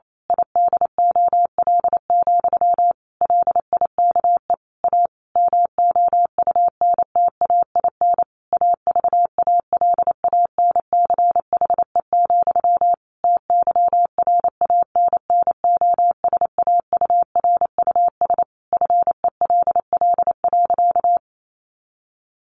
Morse code, heard as a keyed tone, about 28 wpm, 700 Hz. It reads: IDOL, LIKE A MOUNTAIN AVALANCHE, TYRANNOSAURUS FELL.
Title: A Sound Of Thunder 28wpm